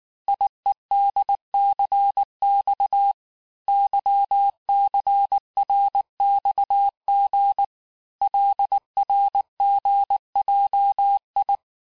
Morse code, heard as a keyed tone, about 19 wpm, 800 Hz. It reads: IEDCX YCRXG LRGJI
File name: cours-du-lundi files / cours-1589-texte-chiffre-vitesse-18mpm